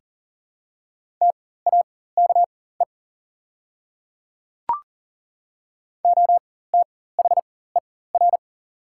Morse code, heard as a keyed tone, about 38 wpm, spaced as for 18 wpm, 700 Hz. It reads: TAKE OTHER
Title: 4x Character Spacing - Top 300 Words - 40wpm